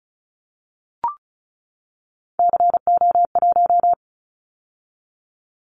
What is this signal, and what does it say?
Morse code, heard as a keyed tone, about 36 wpm, 700 Hz. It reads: CO1
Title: Callsign Prefixes for Top 100 DXCC - 35wpm